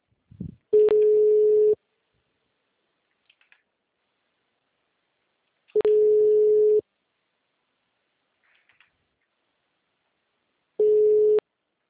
Ringback tone (Europe, 425 Hz).